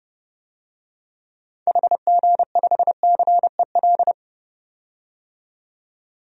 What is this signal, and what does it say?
Morse code, heard as a keyed tone, about 30 wpm, 700 Hz. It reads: HG5CEL
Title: DXCC-100 Call Signs - 6 Characters - 30wpm